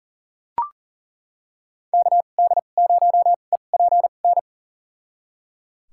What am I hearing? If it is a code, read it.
Morse code, heard as a keyed tone, about 40 wpm, spaced as for 29 wpm, 700 Hz. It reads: KD0EPN
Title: USA Call Signs - 6 Characters - 2x Character Spacing - 40wpm